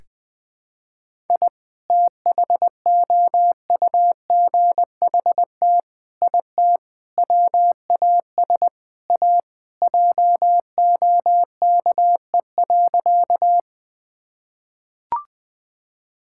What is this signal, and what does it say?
Morse code, heard as a keyed tone, about 20 wpm, 700 Hz. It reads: I THOUGHT IT WAS A JOKE.
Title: Sentences with 5-Word Phrases - Spoken Beginning - 20wpm